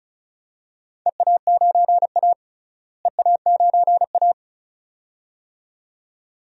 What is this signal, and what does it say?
Morse code, heard as a keyed tone, about 36 wpm, 700 Hz. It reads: EA9A EA9A
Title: DXCC-100 Call Signs - 4 Characters - Repeated 2x - 35wpm